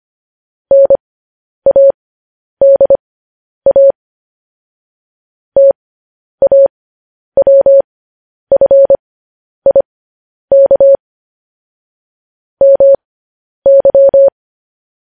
Morse code, heard as a keyed tone, about 25 wpm, spaced as for 10 wpm, 550 Hz. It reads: NADA TAWFIK MY
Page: News Headlines in Morse Code at 10 WPM